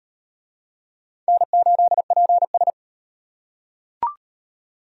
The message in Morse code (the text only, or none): N8PS